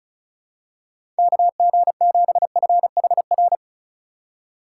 Morse code, heard as a keyed tone, about 36 wpm, 700 Hz. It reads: KG7FHR
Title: USA Call Signs - 6 Characters - 4x Character Spacing - 35wpm